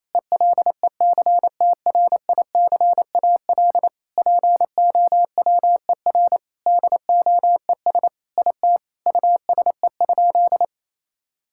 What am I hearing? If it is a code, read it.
Morse code, heard as a keyed tone, about 28 wpm, 700 Hz. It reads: ELECTRICAL POWER DOES IT USE?
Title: Sentences from Top 1000 Words - Review Part 6 - 28wpm